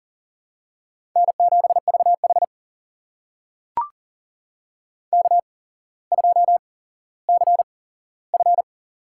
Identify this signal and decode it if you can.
Morse code, heard as a keyed tone, about 40 wpm, spaced as for 17 wpm, 700 Hz. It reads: N7VH K2CF
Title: USA Call Signs - 4 Characters - 8x Character Spacing - 40wpm